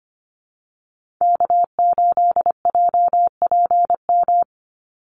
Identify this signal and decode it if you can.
Morse code, heard as a keyed tone, about 25 wpm, 700 Hz. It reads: K8JPM
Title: Call Signs - 25wpm